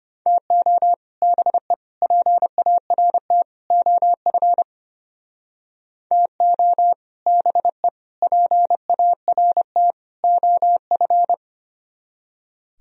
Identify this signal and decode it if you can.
Morse code, heard as a keyed tone, about 33 wpm, 700 Hz. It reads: TO BE PART OF TO BE PART OF